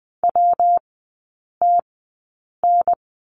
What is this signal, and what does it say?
Morse code, heard as a keyed tone, about 20 wpm, spaced as for 8 wpm, 690 Hz. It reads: WTN
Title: 133 Lesson 22e - Question Mark - Three Characters 20wpm